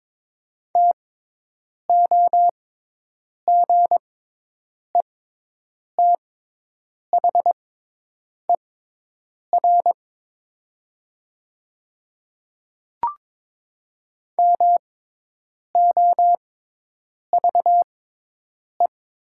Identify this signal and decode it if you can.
Morse code, heard as a keyed tone, about 22 wpm, spaced as for 8 wpm, 700 Hz. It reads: TOGETHER MOVE
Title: 6x Character Spacing - Top 500 Words - 22wpm